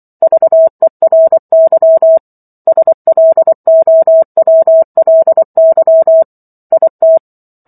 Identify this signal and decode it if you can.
Morse code, heard as a keyed tone, about 24 wpm, 650 Hz. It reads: VERY SLOWLY IT